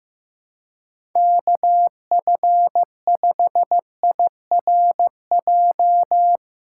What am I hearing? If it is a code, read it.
Morse code, heard as a keyed tone, about 15 wpm, 700 Hz. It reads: KF5IRJ
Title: USA Callsign DE Callsign - 6 Characters - 15wpm